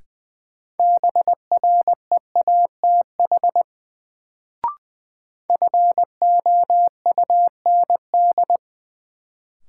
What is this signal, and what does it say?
Morse code, heard as a keyed tone, about 21 wpm, 700 Hz. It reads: BREATH FOUND
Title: Sets of 5 Words Encore - New Word Drill - 20wpm